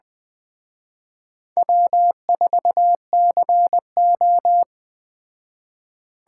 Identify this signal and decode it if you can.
Morse code, heard as a keyed tone, about 20 wpm, 700 Hz. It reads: W4CO
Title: Speed Racing - USA 1x2 Call Signs 17wpm